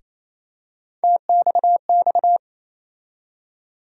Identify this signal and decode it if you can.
Morse code, heard as a keyed tone, about 28 wpm, 700 Hz. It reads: TXX
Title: USA Call Sign Postfixes - 3 Letters 28wpm